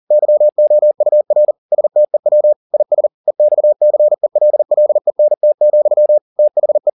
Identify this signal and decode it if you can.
Morse code, heard as a keyed tone, about 40 wpm, 600 Hz. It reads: YOUR STEW IS EXCELLENT, THE